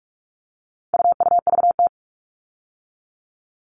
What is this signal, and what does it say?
Morse code, heard as a keyed tone, about 44 wpm, 700 Hz. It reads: UUVT
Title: All Cut Numbers - 4-Digit 45wpm